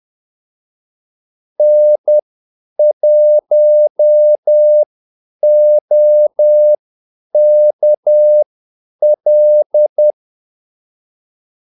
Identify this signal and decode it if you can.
Morse code, heard as a keyed tone, about 10 wpm, spaced as for 8 wpm, 600 Hz. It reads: N1OKL
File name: Callsigns-10-8wpm